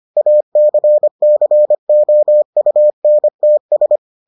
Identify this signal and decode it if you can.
Morse code, heard as a keyed tone, about 25 wpm, 600 Hz. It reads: ACCOUNTS